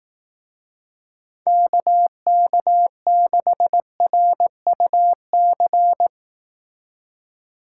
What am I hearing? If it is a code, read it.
Morse code, heard as a keyed tone, about 18 wpm, 700 Hz. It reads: KK6RUC
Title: Speed Racing - USA Call Signs 15wpm